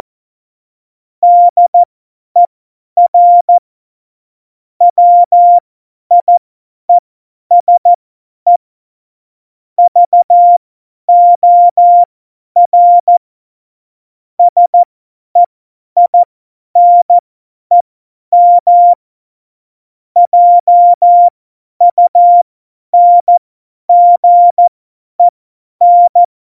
Morse code, heard as a keyed tone, about 14 wpm, spaced as for 10 wpm, 700 Hz. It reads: DER WIESE VOR SEINEM JUNGEN
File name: Morsekurs / Witze / Witz 17. November 2020, 50 F70